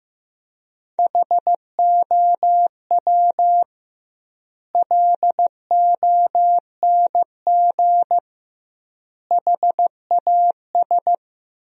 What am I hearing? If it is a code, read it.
Morse code, heard as a keyed tone, about 15 wpm, 700 Hz. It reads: HOW LONG HAS
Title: 2x Word-Spacing - Sentences from Top 100 Words - 15wpm